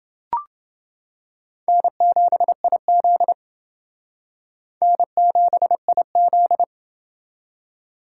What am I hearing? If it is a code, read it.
Morse code, heard as a keyed tone, about 30 wpm, 700 Hz. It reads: N7IZ N7IZ